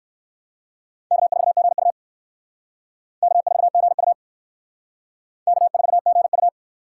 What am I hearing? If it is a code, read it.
Morse code, heard as a keyed tone, about 67 wpm, 700 Hz. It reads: K4GU K4GU K4GU